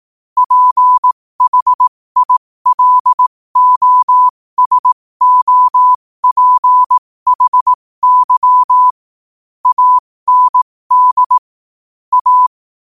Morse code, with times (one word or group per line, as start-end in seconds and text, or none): 0.4-8.9 PHILOSOPHY
9.7-11.4 AND
12.1-12.5 A